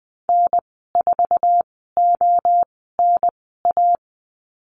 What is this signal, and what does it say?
Morse code, heard as a keyed tone, about 20 wpm, spaced as for 15 wpm, 700 Hz. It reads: N4ONA
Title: 048 Lesson 9g - Number 4 - Call Signs 20wpm